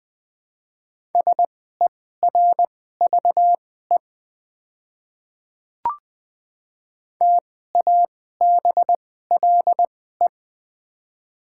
Morse code, heard as a keyed tone, about 20 wpm, spaced as for 14 wpm, 690 Hz. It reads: SERVE TABLE
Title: 2x Character Spacing - Top 500 Words - 20wpm